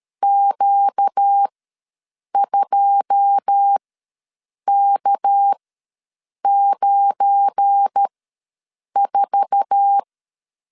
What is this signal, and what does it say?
Morse code, heard as a keyed tone, about 13 wpm, spaced as for 7 wpm, 800 Hz. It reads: Q2K94
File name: LEZIONI / 9 2 ws 12wpm 3406